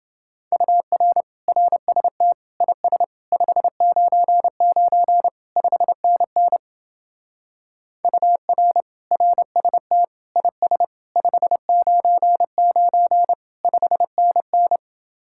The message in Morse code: UR RST IS 599 5NN UR RST IS 599 5NN

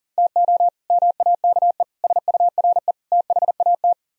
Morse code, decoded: TO MAKE SURE THAT

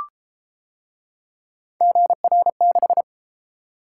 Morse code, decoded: GR6

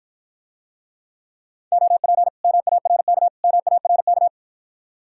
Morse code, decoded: OP MARK MARK

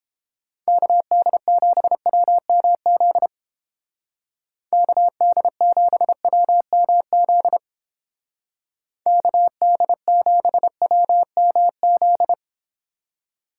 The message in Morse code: KD7WMZ KD7WMZ KD7WMZ

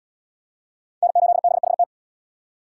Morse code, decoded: N8BHT

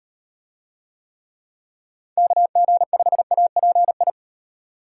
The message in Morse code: KG5API

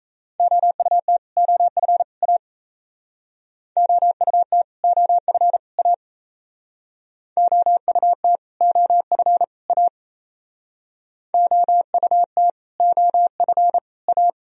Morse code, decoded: OUT OF A OUT OF A OUT OF A OUT OF A

